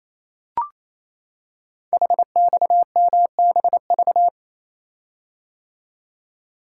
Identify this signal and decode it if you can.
Morse code, heard as a keyed tone, about 28 wpm, 700 Hz. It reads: HXMBV
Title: Code Groups - 5 Letters - 28wpm